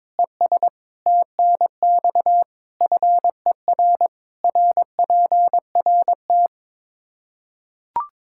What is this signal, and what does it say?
Morse code, heard as a keyed tone, about 21 wpm, 700 Hz. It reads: ES TNX FER RPRT
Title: Practice QSO - Exchange 2 - Info - 22wpm